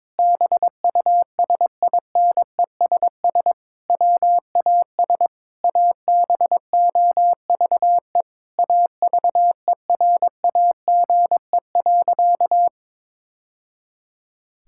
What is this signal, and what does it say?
Morse code, heard as a keyed tone, about 22 wpm, 700 Hz. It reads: BUSINESS WAS ABOVE AVERAGE.
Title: Sentences from Top 1000 Words - Review Part 6 - 22wpm